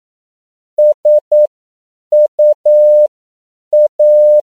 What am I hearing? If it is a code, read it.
Morse code, heard as a keyed tone, about 9 wpm, spaced as for 7 wpm, 600 Hz. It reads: SUA
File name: CW / 10PPM